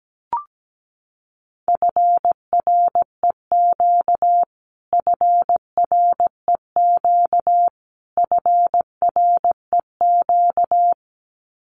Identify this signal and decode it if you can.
Morse code, heard as a keyed tone, about 17 wpm, 700 Hz. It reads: FREQ FREQ FREQ
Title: Abbreviations Repeated 3x - 17wpm